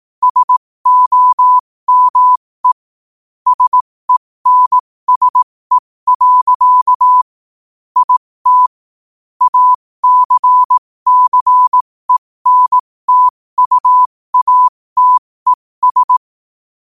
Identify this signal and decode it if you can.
Morse code, heard as a keyed tone, about 18 wpm, spaced as for 15 wpm, 1000 Hz. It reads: SOME SENSE. IT ACCENTUATES
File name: qotdfiles / 15wpm